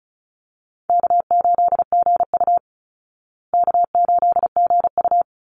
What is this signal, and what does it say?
Morse code, heard as a keyed tone, about 34 wpm, 700 Hz. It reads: K8GU K8GU
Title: Easy Call Signs Repeated 2x - 35wpm